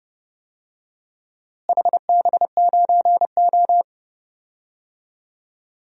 Morse code, heard as a keyed tone, about 31 wpm, 700 Hz. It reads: HB9O
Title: DXCC-100 Call Signs - 4 Characters - Repeated 2x - 30wpm